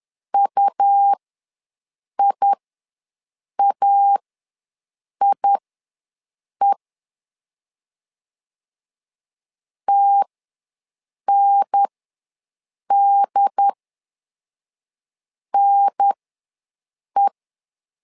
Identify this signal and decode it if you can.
Morse code, heard as a keyed tone, about 10 wpm, spaced as for 6 wpm, 800 Hz. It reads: UIAIE TNDNE
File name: 2 UHDN ws 10wpm